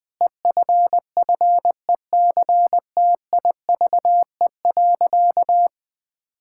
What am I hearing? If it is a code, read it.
Morse code, heard as a keyed tone, about 20 wpm, 700 Hz. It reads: EFFECTIVE.
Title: Sentences from Top 1000 Words - Review Part 6 - 20wpm